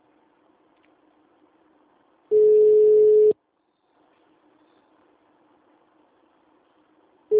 Ringback tone (Europe, 425 Hz).
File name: out- -2008- - - .5397